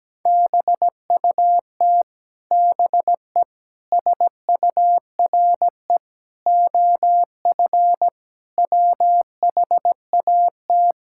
Morse code, heard as a keyed tone, about 17 wpm, 700 Hz. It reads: BUT BE SURE OF WHAT
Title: Sentences from Top 1600 Words 17wpm